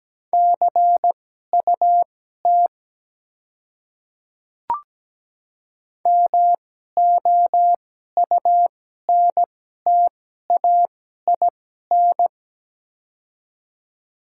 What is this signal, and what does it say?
Morse code, heard as a keyed tone, about 17 wpm, spaced as for 12 wpm, 700 Hz. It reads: CUT MOUNTAIN
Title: 2x Character Spacing - Top 300 Words - 17wpm